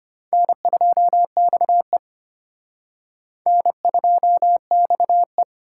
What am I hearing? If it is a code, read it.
Morse code, heard as a keyed tone, about 30 wpm, 700 Hz. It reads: N2XE N2XE
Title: Speed Racing - USA 1x2 Call Signs 25wpm